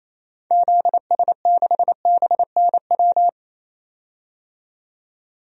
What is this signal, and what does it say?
Morse code, heard as a keyed tone, about 28 wpm, 700 Hz. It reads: ZS6BNW
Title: DXCC-100 Call Signs - 6 Characters - 28wpm